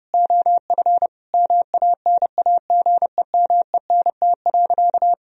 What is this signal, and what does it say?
Morse code, heard as a keyed tone, about 30 wpm, 700 Hz. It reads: OF MANAGEMENT.
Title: Sentences from Top 1000 Words - Review Part 4 - 30wpm